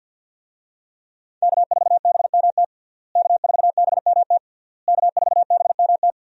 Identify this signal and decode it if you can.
Morse code, heard as a keyed tone, about 52 wpm, 700 Hz. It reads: K4BMT K4BMT K4BMT
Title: USA Call Signs - 5 Characters - Repeated 3x - 50wpm